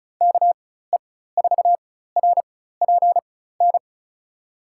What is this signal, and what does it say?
Morse code, heard as a keyed tone, about 36 wpm, spaced as for 16 wpm, 700 Hz. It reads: KE4RPN
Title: USA Call Signs - 6 Characters - 4x Character Spacing - 35wpm